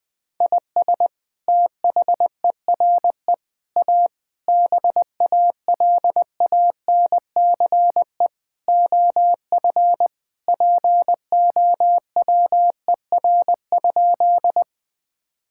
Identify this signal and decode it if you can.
Morse code, heard as a keyed tone, about 20 wpm, 700 Hz. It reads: IS THERE A BALANCE OF POWER?
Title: Sentences from Top 1000 Words - Review Part 6 - 20wpm